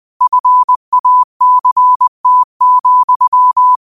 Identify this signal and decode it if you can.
Morse code, heard as a keyed tone, about 20 wpm, 1000 Hz. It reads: FACT,
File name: qotdfiles / 20wpm